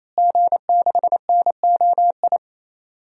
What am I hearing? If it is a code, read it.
Morse code, heard as a keyed tone, about 28 wpm, 700 Hz. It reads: G6NOI
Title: DXCC-100 Call Signs - 5 Characters - Repeated 3x - 28wpm